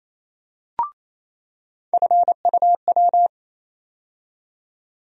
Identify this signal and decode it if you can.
Morse code, heard as a keyed tone, about 28 wpm, 700 Hz. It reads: FUW
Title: USA Call Sign Postfixes - 3 Letters 28wpm